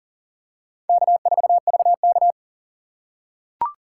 Morse code, heard as a keyed tone, about 40 wpm, 700 Hz. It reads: K4VK
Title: USA Call Signs - 4x Character Spacing - 40wpm